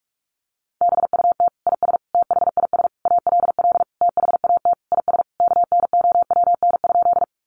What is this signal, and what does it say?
Morse code, heard as a keyed tone, about 44 wpm, 700 Hz. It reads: BUT IS THIS ALL THAT IS KNOWN?
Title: Sentences from Top 600 Words 45wpm